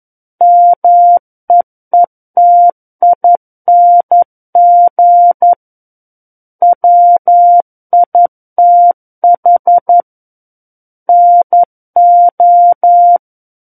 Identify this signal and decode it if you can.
Morse code, heard as a keyed tone, about 11 wpm, 700 Hz. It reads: MEETING WITH NO
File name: 11 WPM